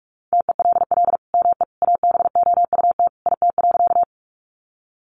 Morse code, heard as a keyed tone, about 44 wpm, 700 Hz. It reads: TELL ME ABOUT IT.
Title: Sentences from Top 200 Words 45wpm